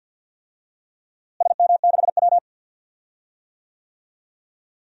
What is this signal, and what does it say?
Morse code, heard as a keyed tone, about 49 wpm, 700 Hz. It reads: SM6W
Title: DXCC-100 Call Signs - 4 Characters - Repeated 3x - 50wpm